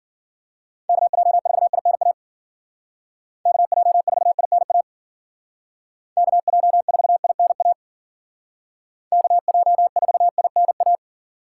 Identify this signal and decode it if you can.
Morse code, heard as a keyed tone, about 51 wpm, 700 Hz. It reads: KJ4INA KJ4INA KJ4INA KJ4INA